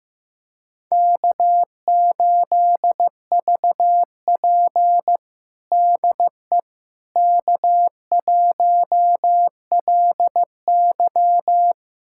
Morse code, heard as a keyed tone, about 15 wpm, 700 Hz. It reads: K8VP DE K1LY
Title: USA Callsign DE Callsign - 4 Characters - 15wpm